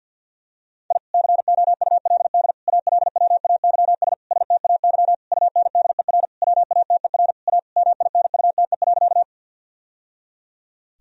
Morse code, heard as a keyed tone, about 51 wpm, 700 Hz. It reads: I COULD ALWAYS STAY UNDER WATER A MINUTE.
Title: Sentences from Top 1000 Words - Review Part 1 - 50wpm